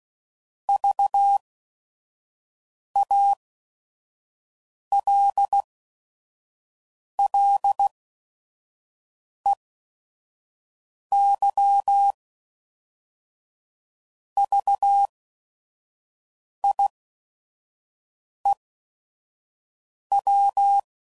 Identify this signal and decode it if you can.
Morse code, heard as a keyed tone, about 16 wpm, spaced as for 5 wpm, 800 Hz. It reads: VALLEYVIEW
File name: CW Code Groups @5WPM / fromSM's / Cities